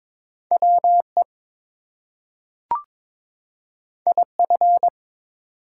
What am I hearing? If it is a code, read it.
Morse code, heard as a keyed tone, about 22 wpm, 700 Hz. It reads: WE IF
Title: Two Letter Words - 22wpm